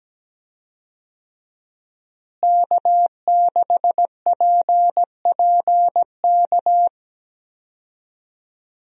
Morse code, heard as a keyed tone, about 17 wpm, 700 Hz. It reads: K6PPK
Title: USA Call Signs - 5 Characters - Repeated 3x - 17wpm